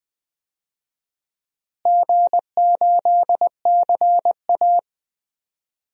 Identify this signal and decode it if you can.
Morse code, heard as a keyed tone, about 21 wpm, 700 Hz. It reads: G8CA
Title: DXCC-100 Call Signs - 4 Characters - Repeated 2x - 20wpm